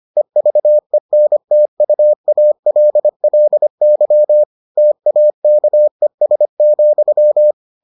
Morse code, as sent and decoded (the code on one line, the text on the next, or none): . ...- . -. - ..- .- .-.. .-.. -.-- / - .- -.- . ... --..--
EVENTUALLY TAKES,